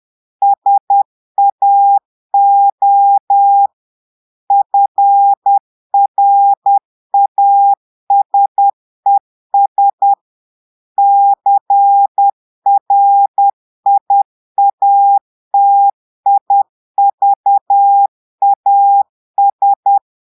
Morse code, heard as a keyed tone, about 10 wpm, 800 Hz. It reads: SAO FRASES CRIATIVAS
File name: portugais texte en clair 2 vitesse 10 mots minutes